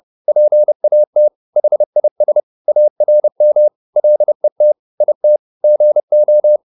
Morse code, heard as a keyed tone, about 30 wpm, 600 Hz. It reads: PAT HIS ARM LET IT GO